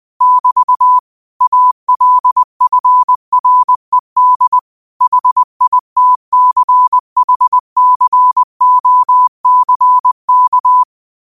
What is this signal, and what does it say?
Morse code, heard as a keyed tone, about 20 wpm, 1000 Hz. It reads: <BT> ALFRED HITCHCOCK
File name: 20wpm